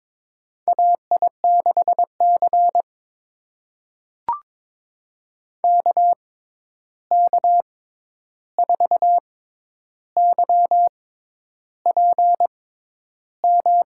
Morse code, heard as a keyed tone, about 22 wpm, spaced as for 10 wpm, 700 Hz. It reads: AI6C KK4YPM
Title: USA Call Signs - 6x Character Spacing - 22wpm